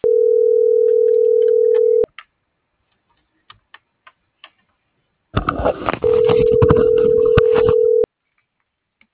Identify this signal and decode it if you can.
Ringback tone (North America), 440 + 480 Hz.